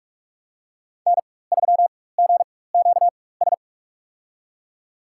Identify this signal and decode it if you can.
Morse code, heard as a keyed tone, about 46 wpm, spaced as for 21 wpm, 700 Hz. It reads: N3GQS
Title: USA Call Signs - 5 Characters - 4x Character Spacing - 45wpm